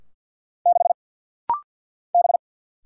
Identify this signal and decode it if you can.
Morse code, heard as a keyed tone, about 50 wpm, 700 Hz. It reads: 6 B
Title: Single Letter-Number - Rapid-Fire 50wpm